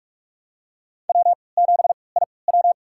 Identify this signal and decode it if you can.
Morse code, heard as a keyed tone, about 46 wpm, spaced as for 26 wpm, 700 Hz. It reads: W7IW
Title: USA Call Signs - 4 Characters - 3x Character Spacing - 45wpm